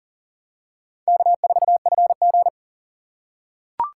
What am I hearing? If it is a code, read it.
Morse code, heard as a keyed tone, about 37 wpm, 700 Hz. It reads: K4FG
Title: USA Call Signs - 4 Characters - 2x Character Spacing - 40wpm